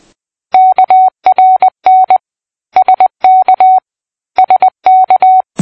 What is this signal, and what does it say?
Morse code, heard as a keyed tone, about 20 wpm, 750 Hz. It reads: KRN SK SK
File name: PI4AA Uitzending